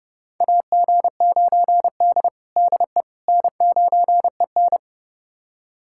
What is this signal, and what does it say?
Morse code, heard as a keyed tone, about 32 wpm, 700 Hz. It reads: AG9D DE N9EN
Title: USA Callsign DE Callsign - 4 Characters - 30wpm